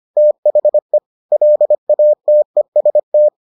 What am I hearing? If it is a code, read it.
Morse code, heard as a keyed tone, about 25 wpm, 600 Hz. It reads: THE LATEST